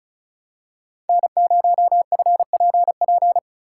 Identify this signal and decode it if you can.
Morse code, heard as a keyed tone, about 36 wpm, 700 Hz. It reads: N0FPP
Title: USA Call Signs - 5 Characters - 4x Character Spacing - 35wpm